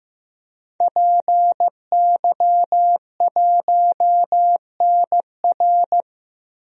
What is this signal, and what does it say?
Morse code, heard as a keyed tone, about 15 wpm, 700 Hz. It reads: PY1NR
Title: DXCC-100 Call Signs - 5 Characters - 15wpm